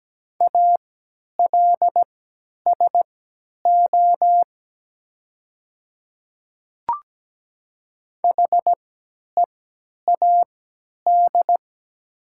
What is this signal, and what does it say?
Morse code, heard as a keyed tone, about 17 wpm, spaced as for 10 wpm, 690 Hz. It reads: ALSO HEAD
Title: 3x Character Spacing - Top 300 Words - 17wpm